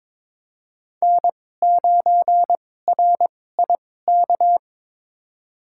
Morse code, heard as a keyed tone, about 22 wpm, spaced as for 16 wpm, 700 Hz. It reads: N9RIK
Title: USA Call Signs - 5 Characters - 2x Character Spacing - 22wpm